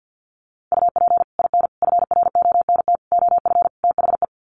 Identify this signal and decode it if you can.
Morse code, heard as a keyed tone, about 50 wpm, 700 Hz. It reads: UP IN FRONT OF THE
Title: Sets of 5 Words Encore2 50wpm